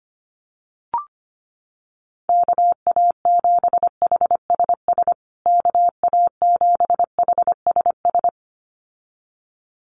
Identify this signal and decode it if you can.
Morse code, heard as a keyed tone, about 25 wpm, 700 Hz. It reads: KA7HSS KA7HSS
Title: USA Call Signs Repeated 2x 25wpm